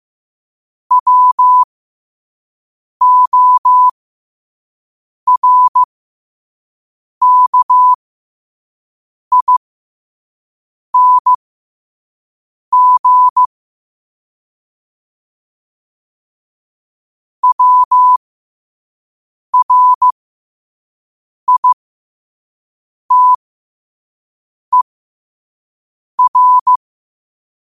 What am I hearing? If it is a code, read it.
Morse code, heard as a keyed tone, about 15 wpm, spaced as for 5 wpm, 1000 Hz. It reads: WORKING WRITER
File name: qotdfiles / 5wpm